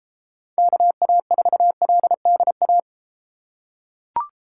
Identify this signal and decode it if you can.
Morse code, heard as a keyed tone, about 33 wpm, 700 Hz. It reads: KA4LDA